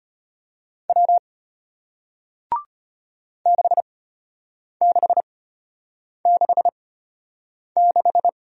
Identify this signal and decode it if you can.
Morse code, heard as a keyed tone, about 34 wpm, spaced as for 8 wpm, 700 Hz. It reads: W 6666